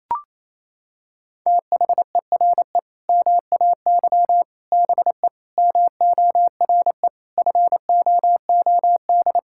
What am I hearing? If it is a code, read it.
Morse code, heard as a keyed tone, about 28 wpm, 700 Hz. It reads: THERE MAY BE MORE FOOD
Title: Sentences from Top 1000 Words - Review Part 6 - 28wpm